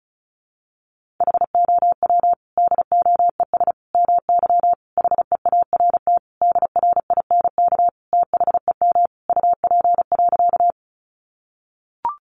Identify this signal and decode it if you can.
Morse code, heard as a keyed tone, about 34 wpm, 700 Hz. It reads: HOW DOES MY HEART DRINK THEM UP.
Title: Sentences from Top 1500 Words 35wpm